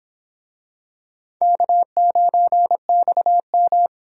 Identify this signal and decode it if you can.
Morse code, heard as a keyed tone, about 26 wpm, 700 Hz. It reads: K9XM